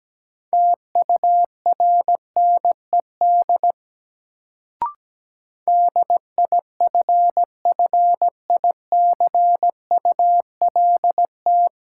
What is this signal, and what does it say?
Morse code, heard as a keyed tone, about 17 wpm, 700 Hz. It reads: TURNED DIFFICULT
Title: Sets of 4 Words - New Word Drill - 17wpm